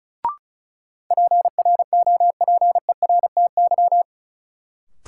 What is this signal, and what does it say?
Morse code, heard as a keyed tone, about 36 wpm, 700 Hz. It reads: PROPERTY
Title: Sentences from Top 1000 Words - New Word Drill - 35wpm